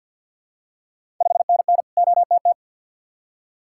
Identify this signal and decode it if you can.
Morse code, heard as a keyed tone, about 51 wpm, 700 Hz. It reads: 5NN OTT
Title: Ontario QSO Party Exchange 50wpm